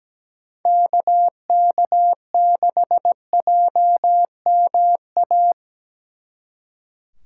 Morse code, heard as a keyed tone, about 19 wpm, 700 Hz. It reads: KK6JMA
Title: Speed Racing - USA 2x3 Call Signs 17wpm